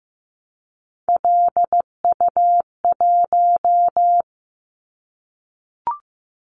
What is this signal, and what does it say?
Morse code, heard as a keyed tone, about 15 wpm, 700 Hz. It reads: LU1
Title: Callsign Prefixes for Top 100 DXCC - 15wpm